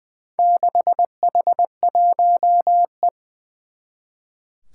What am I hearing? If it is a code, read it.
Morse code, heard as a keyed tone, about 20 wpm, 700 Hz. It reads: 6H1E
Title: Four Challenging Letter-Number Combinations 20wpm